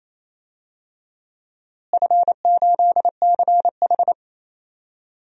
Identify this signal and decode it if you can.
Morse code, heard as a keyed tone, about 28 wpm, 700 Hz. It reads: F8CH
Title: DXCC-100 Call Signs - 4 Characters - Repeated 3x - 28wpm